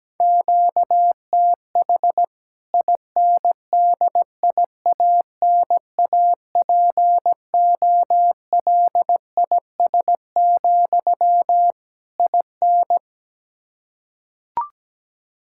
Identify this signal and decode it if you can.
Morse code, heard as a keyed tone, about 17 wpm, 700 Hz. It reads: QTH INDIANAPOLIS, IN
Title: K1USN SST QSO Practice 17wpm